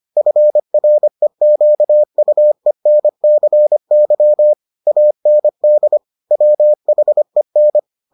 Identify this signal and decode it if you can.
Morse code, heard as a keyed tone, about 25 wpm, 600 Hz. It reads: FREQUENCY AND WHEN